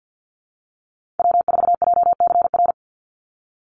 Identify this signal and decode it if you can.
Morse code, heard as a keyed tone, about 50 wpm, 700 Hz. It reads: W4JCR
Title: USA Call Signs 50wpm